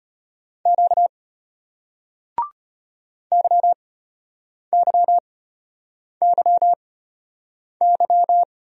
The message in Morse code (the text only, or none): Q YYYY